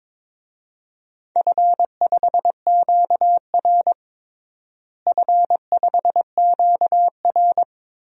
Morse code, heard as a keyed tone, about 22 wpm, 700 Hz. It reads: F5QR F5QR
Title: DXCC-100 Call Signs - 4 Characters - Repeated 2x - 22wpm